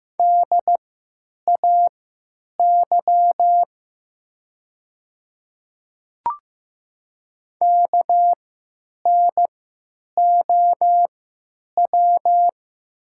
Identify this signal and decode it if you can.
Morse code, heard as a keyed tone, about 15 wpm, spaced as for 9 wpm, 690 Hz. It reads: DAY KNOW
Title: 3x Character Spacing - Top 100 Words - 15wpm